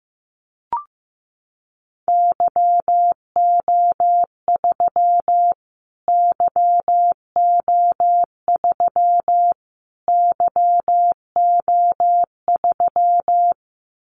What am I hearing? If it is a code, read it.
Morse code, heard as a keyed tone, about 15 wpm, 700 Hz. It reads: YO3 YO3 YO3
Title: Callsign Prefixes for Top 25 DXCC - Repeated 3x - 15wpm